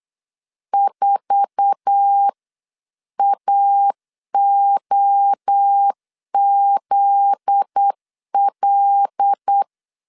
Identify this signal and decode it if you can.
Morse code, heard as a keyed tone, about 8 wpm, 800 Hz. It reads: 4AOZL
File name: CW / DET / 1 9 1500C S C 8wpm 5653